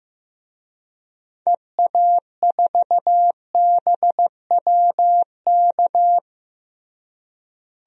Morse code, heard as a keyed tone, about 16 wpm, 700 Hz. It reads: EA4BWK
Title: DXCC-100 Call Signs - 6 Characters - Repeated 2x - 15wpm